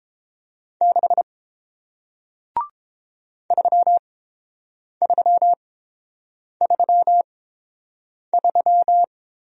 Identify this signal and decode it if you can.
Morse code, heard as a keyed tone, about 30 wpm, spaced as for 7 wpm, 700 Hz. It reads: 6 3333